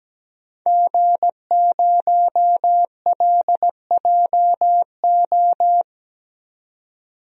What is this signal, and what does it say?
Morse code, heard as a keyed tone, about 17 wpm, 700 Hz. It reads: G0LJO
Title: DXCC-100 Call Signs - 5 Characters - 17wpm